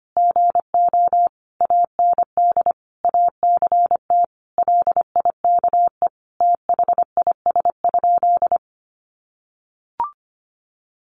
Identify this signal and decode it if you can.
Morse code, heard as a keyed tone, about 25 wpm, 700 Hz. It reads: GO AND ACT LIKE THIS?
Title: Sentences from Top 1200 Words 25wpm